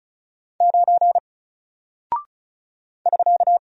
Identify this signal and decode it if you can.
Morse code, heard as a keyed tone, about 36 wpm, 700 Hz. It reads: 9 <SK>
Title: Single Character (Letter:Number:Punctuation:Prosign) Alt 35wpm